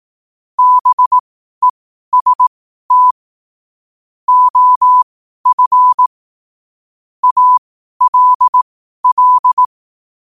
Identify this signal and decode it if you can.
Morse code, heard as a keyed tone, about 18 wpm, spaced as for 12 wpm, 1000 Hz. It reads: BEST OF ALL
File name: qotdfiles / 12wpm